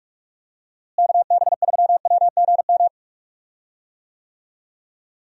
Morse code, heard as a keyed tone, about 46 wpm, 700 Hz. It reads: KB3WGM
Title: USA Call Signs - 6 Characters - Repeated 3x - 45wpm